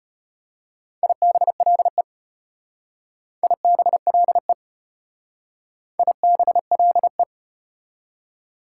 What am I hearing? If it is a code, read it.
Morse code, heard as a keyed tone, about 32 wpm, 690 Hz. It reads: IBLE IBLE IBLE